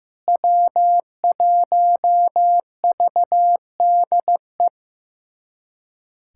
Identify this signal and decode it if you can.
Morse code, heard as a keyed tone, about 16 wpm, 700 Hz. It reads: W1VDE
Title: USA Callsign DE Callsign - 5 Characters - 15wpm